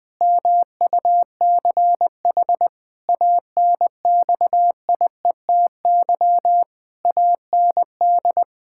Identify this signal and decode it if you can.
Morse code, heard as a keyed tone, about 21 wpm, 700 Hz. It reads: MUCH ANXIETY AND
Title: Pangram Challenge v2 - 20wpm